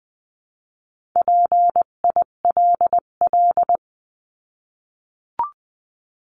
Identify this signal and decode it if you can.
Morse code, heard as a keyed tone, about 20 wpm, 700 Hz. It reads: PILL
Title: 258 Lesson 42c - Words 16-20wpm